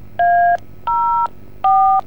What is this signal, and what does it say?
DTMF, heard as telephone touch tones, tones of about 380 ms. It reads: A04